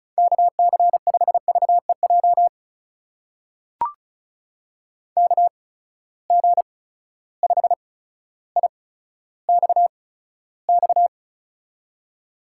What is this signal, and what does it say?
Morse code, heard as a keyed tone, about 36 wpm, spaced as for 15 wpm, 700 Hz. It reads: KC5VEJ KG5IXX
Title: USA Call Signs - 6 Characters - 8x Character Spacing - 35wpm